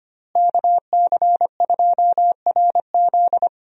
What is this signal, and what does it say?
Morse code, heard as a keyed tone, about 25 wpm, 700 Hz. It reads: KC2RZ
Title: USA Call Signs - 5 Characters - Repeated 2x - 25wpm